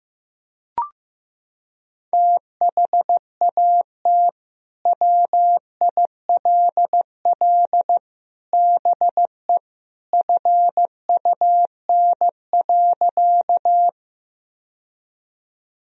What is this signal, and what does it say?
Morse code, heard as a keyed tone, about 15 wpm, 700 Hz. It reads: THAT WILL BE FUN.
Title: Sentences from Top 1000 Words - Review Part 4 - 15wpm